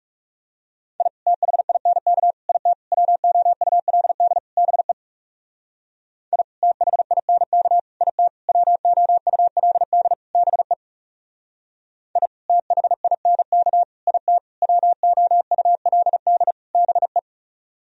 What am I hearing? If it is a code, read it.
Morse code, heard as a keyed tone, about 36 wpm, 700 Hz. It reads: I THINK IT WOULD BE I THINK IT WOULD BE I THINK IT WOULD BE
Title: Speed-Racing - Sets of 5 Words 30wpm